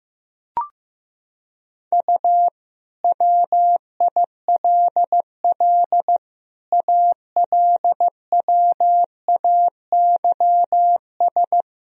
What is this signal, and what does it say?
Morse code, heard as a keyed tone, about 15 wpm, 700 Hz. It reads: U WILL ALWAYS
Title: Sentences from Top 400 Words and Abbreviations 15wpm